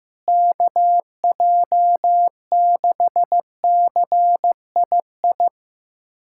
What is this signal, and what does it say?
Morse code, heard as a keyed tone, about 15 wpm, 700 Hz. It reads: KJ6CII
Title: USA Callsign DE Callsign - 6 Characters - 15wpm